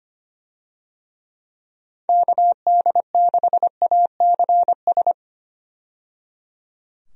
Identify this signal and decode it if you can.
Morse code, heard as a keyed tone, about 25 wpm, 700 Hz. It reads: KD6ACS
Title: USA Call Signs - 6 Characters - 25wpm